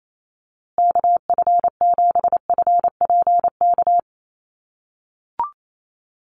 Morse code, heard as a keyed tone, about 28 wpm, 700 Hz. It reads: KF7FPK
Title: USA Call Signs Repeated 2x 28wpm